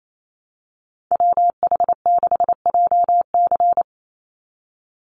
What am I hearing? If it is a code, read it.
Morse code, heard as a keyed tone, about 28 wpm, 700 Hz. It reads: WH6JC
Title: USA Call Signs 28wpm